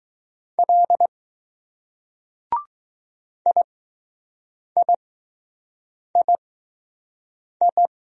Morse code, heard as a keyed tone, about 20 wpm, spaced as for 6 wpm, 700 Hz. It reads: L IIII